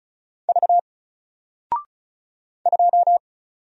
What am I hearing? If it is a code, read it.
Morse code, heard as a keyed tone, about 36 wpm, 700 Hz. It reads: V 2
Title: Single Letter-Number Alt 35wpm